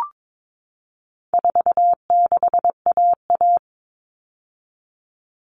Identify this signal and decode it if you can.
Morse code, heard as a keyed tone, about 22 wpm, 700 Hz. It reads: 46AA
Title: All Cut Numbers - 4-Digit 22wpm